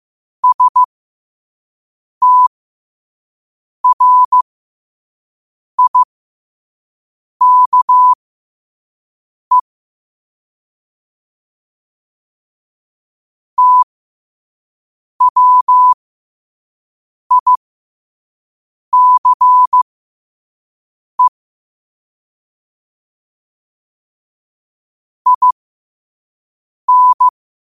Morse code, heard as a keyed tone, about 15 wpm, spaced as for 5 wpm, 1000 Hz. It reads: STRIKE TWICE IN
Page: qotd-at-5-wpm